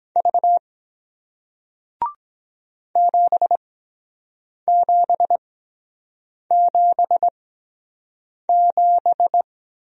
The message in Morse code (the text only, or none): V 7777